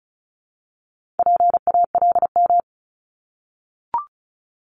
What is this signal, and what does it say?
Morse code, heard as a keyed tone, about 36 wpm, 700 Hz. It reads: PALM
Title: Multi-letter Phonemes Repeated 3x 35wpm